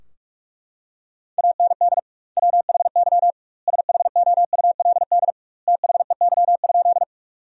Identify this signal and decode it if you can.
Morse code, heard as a keyed tone, about 41 wpm, 700 Hz. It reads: AND WHY SHOULD THEY?
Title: 2x Word-Spacing - Sentences from Top 300 Words - 45wpm